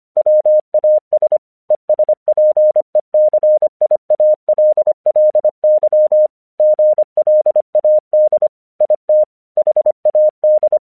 Morse code, heard as a keyed tone, about 25 wpm, 610 Hz. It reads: WAS ESPECIALLY GLAD IT HAD
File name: 1 English 25wpm / AQC17